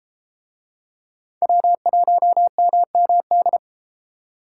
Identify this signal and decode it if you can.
Morse code, heard as a keyed tone, about 27 wpm, 700 Hz. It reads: W1MMD
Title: Speed Racing - USA 1x3 Call Signs 22wpm